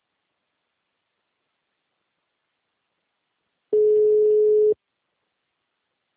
Ringback tone (Europe, 425 Hz).
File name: out- -2013- - - .38974